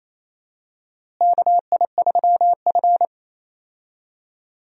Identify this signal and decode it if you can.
Morse code, heard as a keyed tone, about 28 wpm, 700 Hz. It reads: KI3F
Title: USA Call Signs - 4 Characters - Repeated 2x - 28wpm